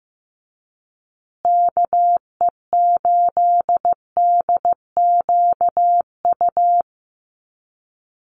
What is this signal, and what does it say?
Morse code, heard as a keyed tone, about 15 wpm, 700 Hz. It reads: KE8DQU
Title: USA Call Signs Repeated 3x 15wpm